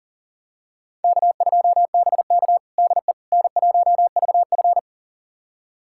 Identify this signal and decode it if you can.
Morse code, heard as a keyed tone, about 37 wpm, 700 Hz. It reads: K2BK DE N1VF
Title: USA Callsign DE Callsign - 4 Characters - 40wpm